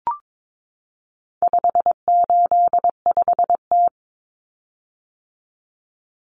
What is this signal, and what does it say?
Morse code, heard as a keyed tone, about 22 wpm, 700 Hz. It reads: 585T
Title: Common Cut Numbers - 4-Digit 22wpm